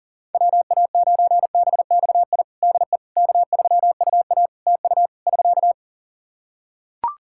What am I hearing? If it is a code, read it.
Morse code, heard as a keyed tone, about 37 wpm, 700 Hz. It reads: WA9BXI DE K3UA TU <SK>
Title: K1USN SST QSO Practice 40wpm